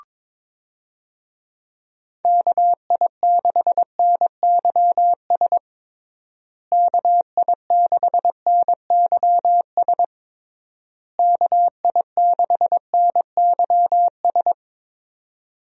Morse code, heard as a keyed tone, about 22 wpm, 700 Hz. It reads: KI6NYS KI6NYS KI6NYS
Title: USA Call Signs - 6 Characters - Repeated 3x - 22wpm